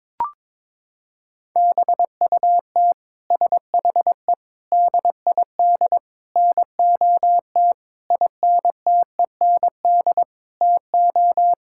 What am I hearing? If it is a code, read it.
Morse code, heard as a keyed tone, about 22 wpm, 700 Hz. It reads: BUT SHE DID NOT INTEND TO
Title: Sentences from Top 1600 Words 22wpm